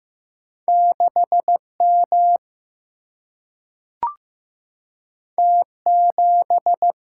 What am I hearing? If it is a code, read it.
Morse code, heard as a keyed tone, about 15 wpm, 700 Hz. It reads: 6 M T 7
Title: Single-Letter Single-Number 15wpm